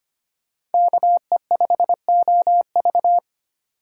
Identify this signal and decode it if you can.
Morse code, heard as a keyed tone, about 33 wpm, 700 Hz. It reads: KE5OV